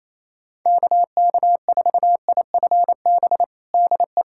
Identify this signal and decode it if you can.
Morse code, heard as a keyed tone, about 28 wpm, 700 Hz. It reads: KK4IFB DE
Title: USA Callsign DE Callsign - 6 Characters - 28wpm